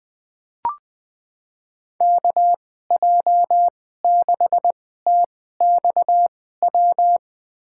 Morse code, heard as a keyed tone, about 20 wpm, spaced as for 15 wpm, 700 Hz. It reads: KJ6TXW
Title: USA Call Signs - 6 Characters - 2x Character Spacing - 20wpm